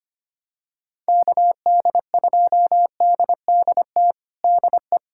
Morse code, heard as a keyed tone, about 25 wpm, 700 Hz. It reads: KD2DDT DE
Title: USA Callsign DE Callsign - 6 Characters - 25wpm